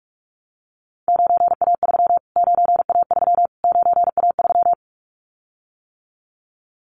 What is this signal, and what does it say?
Morse code, heard as a keyed tone, about 44 wpm, 700 Hz. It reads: 9A3 9A3 9A3
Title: Callsign Prefixes for Top 25 DXCC - Repeated 3x - 45wpm